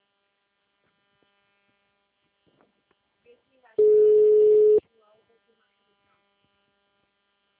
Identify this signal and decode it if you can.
Ringback tone (Europe, 425 Hz), with speech.